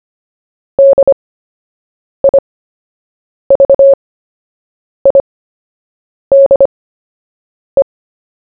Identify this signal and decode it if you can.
Morse code, heard as a keyed tone, about 25 wpm, spaced as for 7 wpm, 550 Hz. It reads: DIVIDE
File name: cw / fists / Short-Story-1-7wpm